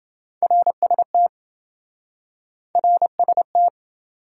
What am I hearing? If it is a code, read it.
Morse code, heard as a keyed tone, about 28 wpm, 700 Hz. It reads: RST RST